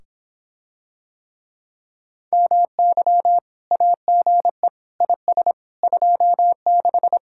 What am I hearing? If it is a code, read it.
Morse code, heard as a keyed tone, about 26 wpm, 700 Hz. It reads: MY AGE IS 26